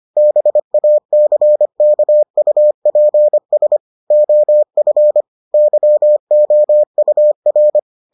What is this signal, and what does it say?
Morse code, heard as a keyed tone, about 25 wpm, 600 Hz. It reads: BACKUPS OF YOUR